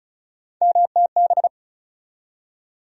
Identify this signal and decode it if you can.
Morse code, heard as a keyed tone, about 36 wpm, 700 Hz. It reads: MTB
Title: USA Call Sign Postfixes - 3 Letters 35wpm